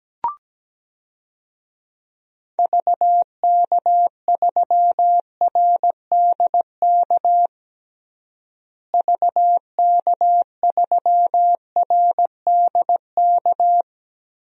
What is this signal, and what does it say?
Morse code, heard as a keyed tone, about 17 wpm, 700 Hz. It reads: VK3RDK VK3RDK
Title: DXCC-100 Call Signs - 6 Characters - Repeated 2x - 17wpm